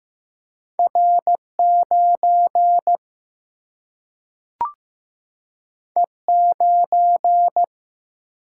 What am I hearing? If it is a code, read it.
Morse code, heard as a keyed tone, about 15 wpm, 700 Hz. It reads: R 9 E 9
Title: Single-Letter Single-Number 15wpm